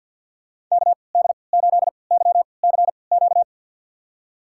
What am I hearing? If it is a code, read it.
Morse code, heard as a keyed tone, about 47 wpm, spaced as for 28 wpm, 700 Hz. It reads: KD8YCQ
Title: USA Call Signs - 3x Character Spacing - 50wpm